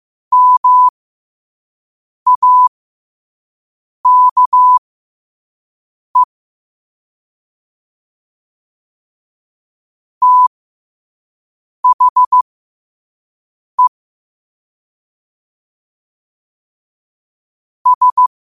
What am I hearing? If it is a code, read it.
Morse code, heard as a keyed tone, about 15 wpm, spaced as for 5 wpm, 1000 Hz. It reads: MAKE THE S